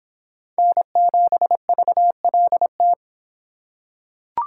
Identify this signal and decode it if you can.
Morse code, heard as a keyed tone, about 26 wpm, 700 Hz. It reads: N7VLT